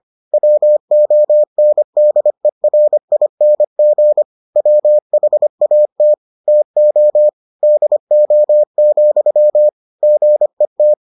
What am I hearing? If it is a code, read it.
Morse code, heard as a keyed tone, about 25 wpm, 600 Hz. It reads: WONDERING WHAT TO DO, GET